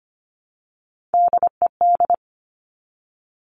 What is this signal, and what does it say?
Morse code, heard as a keyed tone, about 25 wpm, 700 Hz. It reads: DED
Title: Top Three Letter Combinations in Words 25wpm